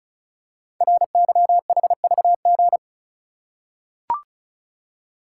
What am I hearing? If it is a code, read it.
Morse code, heard as a keyed tone, about 36 wpm, 690 Hz. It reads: RYHVG
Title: Code Groups - 5 Letters - 35wpm